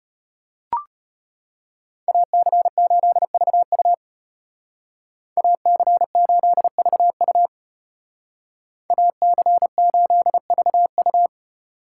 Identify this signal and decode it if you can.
Morse code, heard as a keyed tone, about 32 wpm, 700 Hz. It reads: AC8VU AC8VU AC8VU